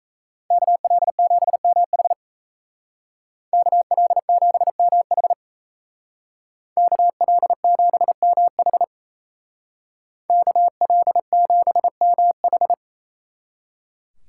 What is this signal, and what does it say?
Morse code, heard as a keyed tone, about 33 wpm, 700 Hz. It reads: KL7MH KL7MH KL7MH KL7MH
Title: Speed Racing - USA 2x2 Call Signs 28wpm